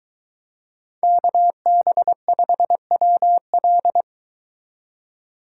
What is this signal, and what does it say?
Morse code, heard as a keyed tone, about 22 wpm, 700 Hz. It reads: KB5WL